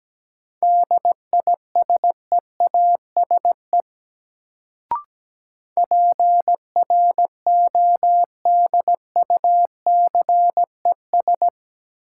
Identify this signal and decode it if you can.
Morse code, heard as a keyed tone, about 17 wpm, 700 Hz. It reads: DISEASE PRODUCES
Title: Sentences from Top 1000 Words - Review Part 6 - New Word Drill - 17wpm